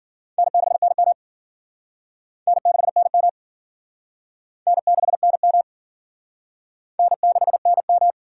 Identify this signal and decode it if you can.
Morse code, heard as a keyed tone, about 50 wpm, 700 Hz. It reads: N6NM N6NM N6NM N6NM
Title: Speed Racing - USA 1x2 Call Signs 40wpm